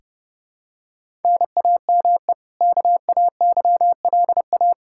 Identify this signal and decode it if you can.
Morse code, heard as a keyed tone, about 32 wpm, 700 Hz. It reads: NAME KAYLA
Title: Practice QSO - Exchange 1 - Beginning - 30wpm